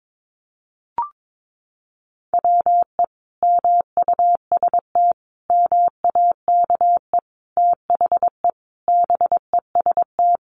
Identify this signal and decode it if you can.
Morse code, heard as a keyed tone, about 22 wpm, 700 Hz. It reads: WE MUST MAKE THE BEST
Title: Sentences from Top 400 Words 22wpm